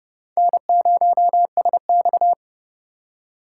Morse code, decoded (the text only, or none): N0SX